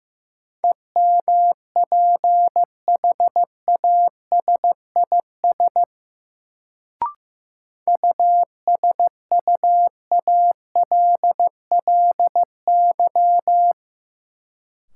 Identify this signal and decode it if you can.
Morse code, heard as a keyed tone, about 15 wpm, 700 Hz. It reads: EMPHASIS USUALLY
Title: Sentences from Top 1400 Words - New Word Drill Alt - 15wpm